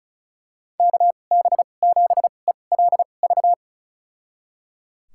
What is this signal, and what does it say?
Morse code, heard as a keyed tone, about 36 wpm, spaced as for 25 wpm, 700 Hz. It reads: KB7ELV
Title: USA Call Signs - 6 Characters - 2x Character Spacing - 35wpm